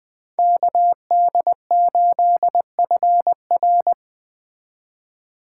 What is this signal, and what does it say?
Morse code, heard as a keyed tone, about 20 wpm, 700 Hz. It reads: KD8FR